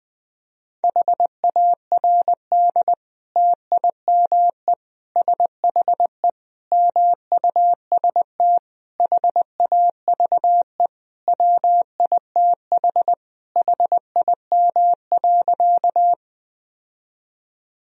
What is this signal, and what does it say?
Morse code, heard as a keyed tone, about 20 wpm, 700 Hz. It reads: HARD TIME SHE MUST HAVE WITH HIM.
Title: Sentences from Top 1000 Words - Review Part 1 - 20wpm